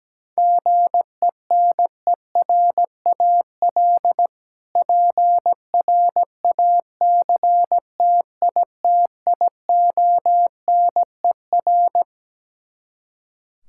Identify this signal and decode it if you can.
Morse code, heard as a keyed tone, about 17 wpm, 700 Hz. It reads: GENERAL PRACTITIONER
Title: K1USN SST QSO Practice 17wpm